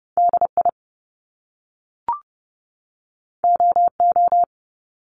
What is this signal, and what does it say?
Morse code, heard as a keyed tone, about 30 wpm, 700 Hz. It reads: DI OO
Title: Top Two Letter Combinations in Words 30wpm